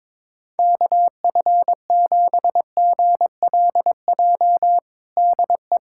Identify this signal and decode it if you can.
Morse code, heard as a keyed tone, about 22 wpm, 700 Hz. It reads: KF7GLJ DE
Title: USA Callsign DE Callsign - 6 Characters - 22wpm